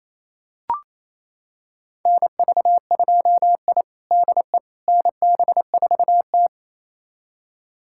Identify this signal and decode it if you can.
Morse code, heard as a keyed tone, about 28 wpm, 700 Hz. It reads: NV2I DE NB4T
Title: USA Callsign DE Callsign - 4 Characters - 28wpm